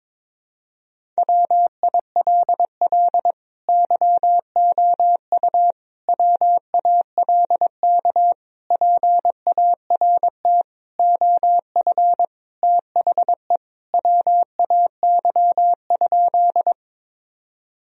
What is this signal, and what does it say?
Morse code, heard as a keyed tone, about 22 wpm, 700 Hz. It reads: WILL YOU WALK PART OF THE WAY?
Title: Sentences from Top 1000 Words - Review Part 1 - 22wpm